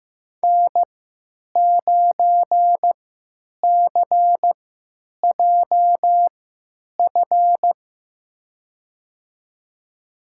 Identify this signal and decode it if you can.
Morse code, heard as a keyed tone, about 15 wpm, spaced as for 9 wpm, 700 Hz. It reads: N9CJF
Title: USA Call Signs - 5 Characters - 3x Character Spacing - 15wpm